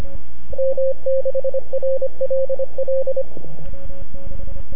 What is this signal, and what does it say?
Morse code, heard as a keyed tone, about 25 wpm, 530 Hz, with radio noise. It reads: M6RLL
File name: NAQPCW-8-2009 / W1UJ NAQP CW 09 0100Z